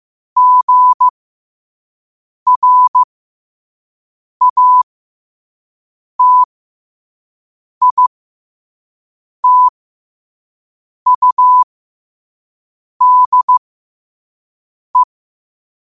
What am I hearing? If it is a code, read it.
Morse code, heard as a keyed tone, about 15 wpm, spaced as for 5 wpm, 1000 Hz. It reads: GRATITUDE
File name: 5wpm